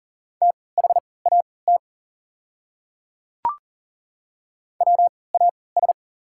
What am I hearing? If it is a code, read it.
Morse code, heard as a keyed tone, about 40 wpm, spaced as for 23 wpm, 690 Hz. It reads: THAT WAS
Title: 3x Character Spacing - Top 100 Words - 40wpm